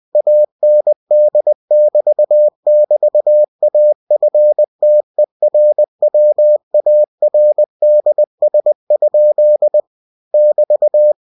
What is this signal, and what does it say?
Morse code, heard as a keyed tone, about 20 wpm, 600 Hz. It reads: AND<BT><BT>AFTERWARDS? <BT>
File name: cw-books / winnie-the-pooh / winnie-20wpm-page-0064